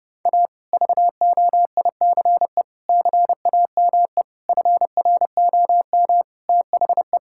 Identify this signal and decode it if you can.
Morse code, heard as a keyed tone, about 32 wpm, 700 Hz. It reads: A VOICE CAME FROM THE
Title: Sentences from Top 1000 Words - Review Part 5 - 30wpm